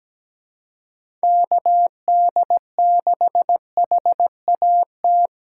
Morse code, heard as a keyed tone, about 17 wpm, 700 Hz. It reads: KD6HAT
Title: USA Call Signs - 6 Characters - 8x Character Spacing - 17wpm